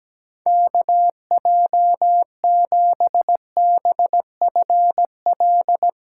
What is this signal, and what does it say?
Morse code, heard as a keyed tone, about 17 wpm, 700 Hz. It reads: KJ7BFL
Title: USA Call Signs - 6 Characters - 8x Character Spacing - 17wpm